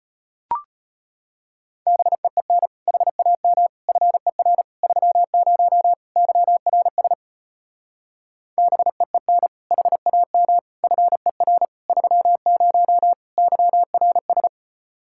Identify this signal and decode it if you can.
Morse code, heard as a keyed tone, about 33 wpm, 700 Hz. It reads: BEEN HAM FER 30 YRS BEEN HAM FER 30 YRS